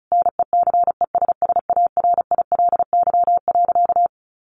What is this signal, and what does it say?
Morse code, heard as a keyed tone, about 34 wpm, 700 Hz. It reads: NECESSARILY.
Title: Sentences from Top 1400 Words 35wpm